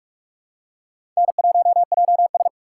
Morse code, heard as a keyed tone, about 43 wpm, 700 Hz. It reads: N1JS
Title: USA Call Signs - 3x Character Spacing - 45wpm